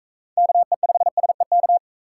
Morse code, heard as a keyed tone, about 44 wpm, 700 Hz. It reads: KE5SEK